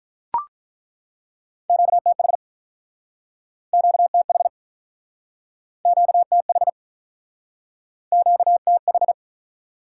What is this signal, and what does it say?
Morse code, heard as a keyed tone, about 47 wpm, 700 Hz. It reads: QTH QTH QTH QTH